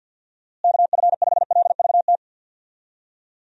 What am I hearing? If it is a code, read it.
Morse code, heard as a keyed tone, about 48 wpm, 700 Hz. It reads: KF5LVT
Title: USA Call Signs - 6 Characters - Repeated 3x - 50wpm